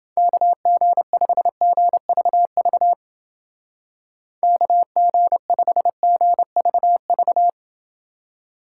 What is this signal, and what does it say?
Morse code, heard as a keyed tone, about 28 wpm, 700 Hz. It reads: KG5GVV KG5GVV